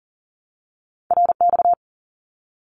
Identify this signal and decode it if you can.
Morse code, heard as a keyed tone, about 40 wpm, 700 Hz. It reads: RX
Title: Abbreviations Repeated 3x - 40wpm